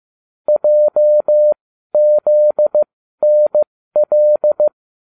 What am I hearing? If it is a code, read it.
Morse code, heard as a keyed tone, about 15 wpm, spaced as for 12 wpm, 600 Hz. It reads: JZNL